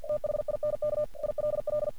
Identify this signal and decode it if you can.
Morse code, heard as a keyed tone, about 50 wpm, 600 Hz. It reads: THINK ILL
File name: archived / CRSnet